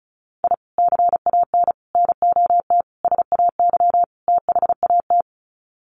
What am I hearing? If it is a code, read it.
Morse code, heard as a keyed tone, about 34 wpm, 700 Hz. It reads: I CAN NOT SAY THAT
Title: Sets of 5 Words Encore2 35wpm